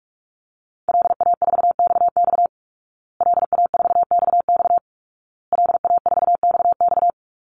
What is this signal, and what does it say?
Morse code, heard as a keyed tone, about 44 wpm, 700 Hz. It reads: LA4XX LA4XX LA4XX
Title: Call Signs Repeated 2x - 45wpm